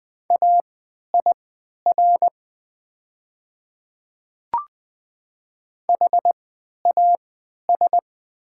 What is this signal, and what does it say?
Morse code, heard as a keyed tone, about 20 wpm, spaced as for 11 wpm, 700 Hz. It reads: AIR HAS
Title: 3x Character Spacing - Top 300 Words - 20wpm